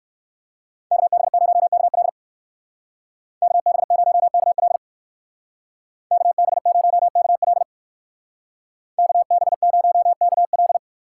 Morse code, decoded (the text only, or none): KB0KL KB0KL KB0KL KB0KL